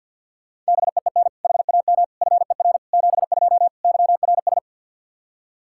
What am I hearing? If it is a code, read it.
Morse code, heard as a keyed tone, about 46 wpm, 700 Hz. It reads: BEEN HAM FER 72 YRS
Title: K1USN SST QSO Practice 50wpm